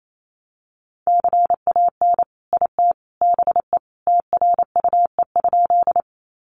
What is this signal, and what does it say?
Morse code, heard as a keyed tone, about 28 wpm, 700 Hz. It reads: CAN IT BE TRUE?
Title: Sentences from Top 400 Words 28wpm